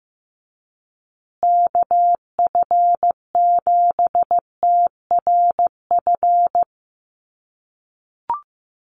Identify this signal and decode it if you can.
Morse code, heard as a keyed tone, about 15 wpm, 700 Hz. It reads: KF7TRF
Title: USA Call Signs Repeated 2x 15wpm